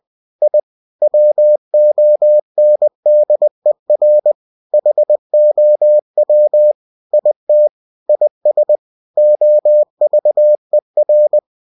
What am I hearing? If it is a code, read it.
Morse code, heard as a keyed tone, about 20 wpm, 600 Hz. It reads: I WONDER HOW IT IS OVER
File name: cw-books / winnie-the-pooh / winnie-20wpm-page-0091